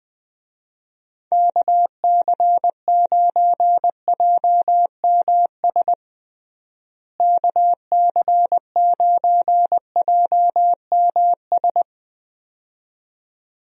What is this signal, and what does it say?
Morse code, heard as a keyed tone, about 21 wpm, 700 Hz. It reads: KC9JMS KC9JMS
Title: USA Call Signs - 6 Characters - Repeated 2x - 20wpm